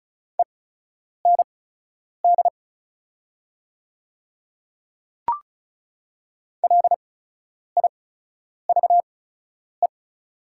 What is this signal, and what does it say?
Morse code, heard as a keyed tone, about 36 wpm, spaced as for 10 wpm, 690 Hz. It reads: END LIVE
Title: 8x Character Spacing - Top 200 Words - 35wpm